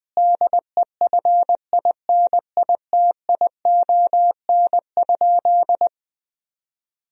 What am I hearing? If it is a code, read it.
Morse code, heard as a keyed tone, about 20 wpm, 700 Hz. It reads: DEFINITION?
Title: Sentences from Top 1000 Words - Review Part 4 - 20wpm